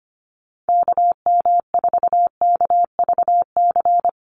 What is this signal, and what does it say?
Morse code, heard as a keyed tone, about 25 wpm, 700 Hz. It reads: KM4KVC
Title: USA Call Signs 25wpm